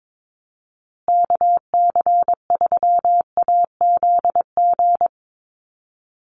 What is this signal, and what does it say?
Morse code, heard as a keyed tone, about 22 wpm, 700 Hz. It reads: KC3AZG
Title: USA Call Signs Repeated 2x 22wpm